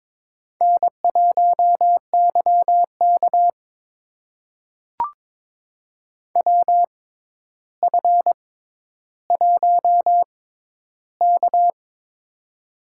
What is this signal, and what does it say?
Morse code, heard as a keyed tone, about 21 wpm, spaced as for 11 wpm, 700 Hz. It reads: N1YK WF1K
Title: USA Call Signs - 4 Characters - 6x Character Spacing - 22wpm